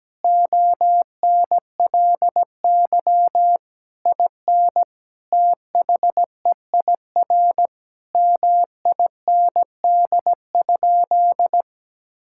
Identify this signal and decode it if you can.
Morse code, heard as a keyed tone, about 17 wpm, 700 Hz. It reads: ONLY IN THEIR MIND?
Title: Sentences from Top 1000 Words - Review Part 5 - 17wpm